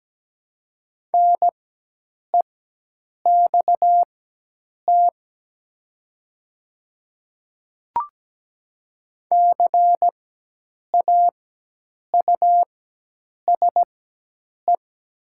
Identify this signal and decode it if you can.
Morse code, heard as a keyed tone, about 17 wpm, spaced as for 8 wpm, 700 Hz. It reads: NEXT CAUSE
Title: 4x Character Spacing - Top 500 Words - 17wpm